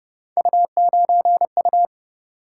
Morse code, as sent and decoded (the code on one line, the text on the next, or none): ..- ----. ..-
U9U